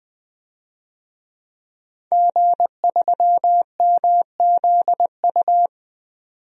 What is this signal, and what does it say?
Morse code, heard as a keyed tone, about 20 wpm, 700 Hz. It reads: G3MZU
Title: DXCC-100 Call Signs - 5 Characters - 20wpm